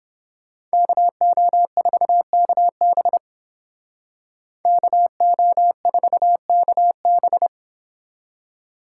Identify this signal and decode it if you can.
Morse code, heard as a keyed tone, about 29 wpm, 700 Hz. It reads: KO4KB KO4KB